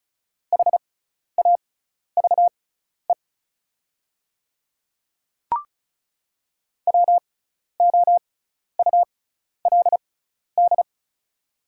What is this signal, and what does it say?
Morse code, heard as a keyed tone, about 36 wpm, spaced as for 12 wpm, 700 Hz. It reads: HAVE WOULD
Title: 6x Character Spacing - Top 100 Words - 35wpm